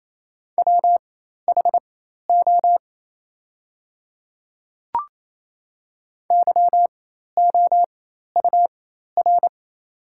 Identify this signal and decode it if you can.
Morse code, heard as a keyed tone, about 28 wpm, spaced as for 13 wpm, 700 Hz. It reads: WHO YOUR
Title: 4x Character Spacing - Top 100 Words - 28wpm